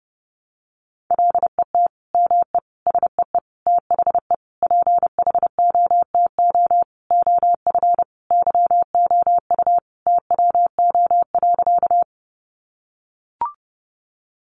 Morse code, heard as a keyed tone, about 30 wpm, 700 Hz. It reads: LET ME SEE THE PHOTO OF YOU TWO.
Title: Sentences from Top 1200 Words 30wpm